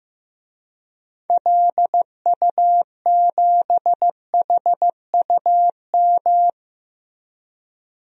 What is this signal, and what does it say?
Morse code, heard as a keyed tone, about 15 wpm, 700 Hz. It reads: LU7HUM
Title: DXCC-100 Call Signs - 6 Characters - Repeated 2x - 15wpm